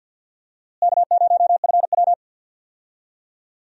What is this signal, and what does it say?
Morse code, heard as a keyed tone, about 51 wpm, 700 Hz. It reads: K0FW
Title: USA Call Signs - 4 Characters - Repeated 2x - 50wpm